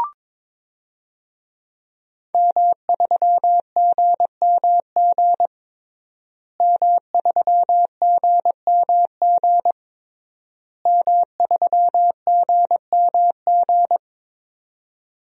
Morse code, heard as a keyed tone, about 22 wpm, 700 Hz. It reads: M3GMG M3GMG M3GMG
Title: DXCC-100 Call Signs - 5 Characters - Repeated 3x - 22wpm